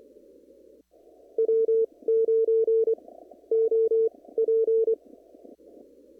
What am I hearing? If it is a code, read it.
Morse code, heard as a keyed tone, about 25 wpm, spaced as for 15 wpm, 450 Hz. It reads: W9OP
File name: CW / kt5j wpx cw-op-k5pi